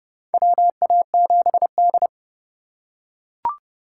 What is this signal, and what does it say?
Morse code, heard as a keyed tone, about 29 wpm, 700 Hz. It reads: WA7D
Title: USA Call Signs - 3x Character Spacing - 30wpm